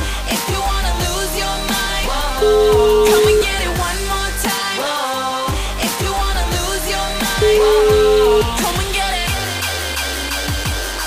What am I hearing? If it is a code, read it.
Ringback tone (Europe, 425 Hz), with noise.